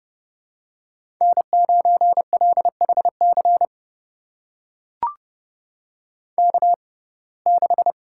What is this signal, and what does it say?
Morse code, heard as a keyed tone, about 30 wpm, spaced as for 22 wpm, 700 Hz. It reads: N9LHC K6
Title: USA Call Signs - 6x Character Spacing - 30wpm